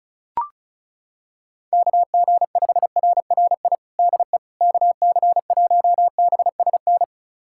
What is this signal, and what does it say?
Morse code, heard as a keyed tone, about 36 wpm, 700 Hz. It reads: KG5RRI DE KC1BSN
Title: USA Callsign DE Callsign 35wpm